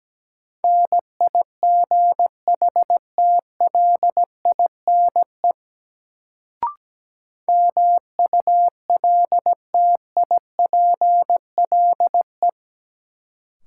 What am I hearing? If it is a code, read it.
Morse code, heard as a keyed tone, about 17 wpm, 700 Hz. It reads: NIGHTLINE MULTIPLE
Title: Sets of 5 Words - New Word Drill - 17wpm